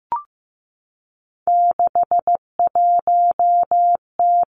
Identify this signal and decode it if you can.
Morse code, heard as a keyed tone, about 15 wpm, 700 Hz. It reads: 61T
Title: Three Challenging Letter-Number Combinations 15wpm